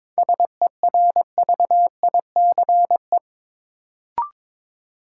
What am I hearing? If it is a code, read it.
Morse code, heard as a keyed tone, about 22 wpm, 700 Hz. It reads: SERVICE
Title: Sentences from Top 700 Words - New Word Drill - 22wpm